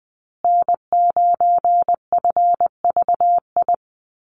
Morse code, heard as a keyed tone, about 20 wpm, 700 Hz. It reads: N9FVI
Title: USA Call Signs Repeated 3x 20wpm